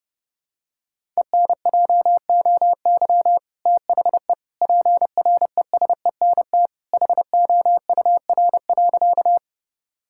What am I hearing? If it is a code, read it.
Morse code, heard as a keyed tone, about 32 wpm, 700 Hz. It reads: ENJOY THE PRESENT HOUR.
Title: Sentences from Top 1000 Words - Review Part 5 - 30wpm